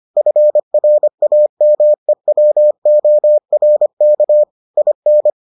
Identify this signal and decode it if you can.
Morse code, heard as a keyed tone, about 25 wpm, 600 Hz. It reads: FRAMEWORK IN